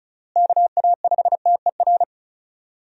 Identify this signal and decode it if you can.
Morse code, heard as a keyed tone, about 34 wpm, 700 Hz. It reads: KA5TER
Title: USA Call Signs - 6 Characters - Repeated 3x - 35wpm